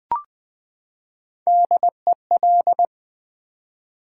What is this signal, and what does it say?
Morse code, heard as a keyed tone, about 20 wpm, 700 Hz. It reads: DEL
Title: Build A Word - 3 Letters At A Time - 8-Letter Words - 20wpm